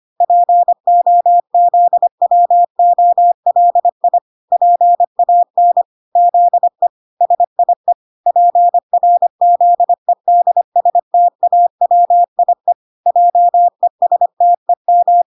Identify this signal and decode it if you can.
Morse code, heard as a keyed tone, about 25 wpm, 700 Hz. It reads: POZWOLI PAN ZE SIE PRZEDSTAWIE JESTEM